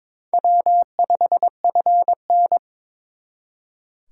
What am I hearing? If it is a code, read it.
Morse code, heard as a keyed tone, about 26 wpm, 700 Hz. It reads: W5FN
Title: Speed Racing - USA 1x2 Call Signs 22wpm